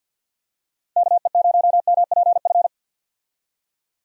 Morse code, heard as a keyed tone, about 51 wpm, 700 Hz. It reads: KE0MPF
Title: USA Call Signs - 6 Characters - 50wpm